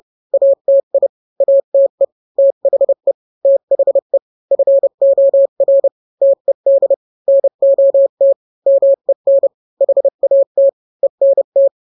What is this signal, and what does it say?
Morse code, heard as a keyed tone, about 30 wpm, 550 Hz. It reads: ATI ATE THE THE FOR TED NOT MEN HAT ENT